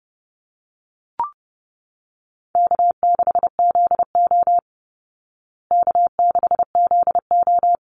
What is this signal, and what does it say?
Morse code, heard as a keyed tone, about 30 wpm, 700 Hz. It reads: K6ZO K6ZO
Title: Easy Call Signs Repeated 2x - 30wpm